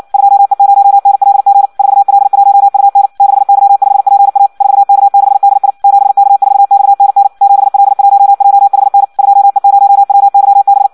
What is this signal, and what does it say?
Morse code, heard as a keyed tone, about 56 wpm, 800 Hz. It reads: 2E0NLG 4L1UN 6Y5PA 4K6DI 7K4XNN 7L1JHN 2E0WYB
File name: Audio practica CW / qrq-tx-58 2